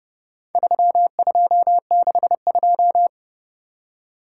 Morse code, heard as a keyed tone, about 29 wpm, 700 Hz. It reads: 3262